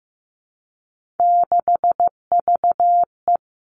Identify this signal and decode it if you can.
Morse code, heard as a keyed tone, about 15 wpm, 700 Hz. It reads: 6VE
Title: Three Challenging Letter-Number Combinations 15wpm